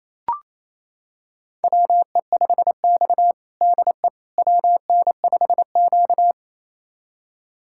Morse code, heard as a keyed tone, about 28 wpm, 700 Hz. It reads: WE5X DE WN5Q
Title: USA Callsign DE Callsign - 4 Characters - 28wpm